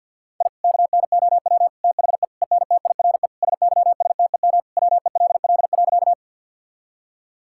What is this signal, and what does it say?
Morse code, heard as a keyed tone, about 50 wpm, 700 Hz. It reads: I KNOW THE ENTIRE SYSTEM WELL.
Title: Sentences from Top 1000 Words - Review Part 5 - 50wpm